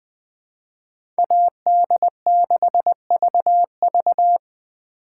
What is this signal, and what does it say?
Morse code, heard as a keyed tone, about 20 wpm, 700 Hz. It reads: AD6VV
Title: USA Call Signs - 3x Character Spacing - 20wpm